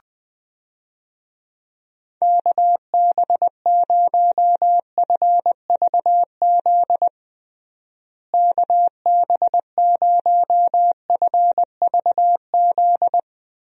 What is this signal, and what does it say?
Morse code, heard as a keyed tone, about 20 wpm, 700 Hz. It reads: KB0FVZ KB0FVZ
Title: USA Call Signs - 6 Characters - Repeated 3x - 20wpm